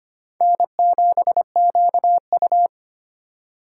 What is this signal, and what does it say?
Morse code, heard as a keyed tone, about 32 wpm, 700 Hz. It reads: N7QU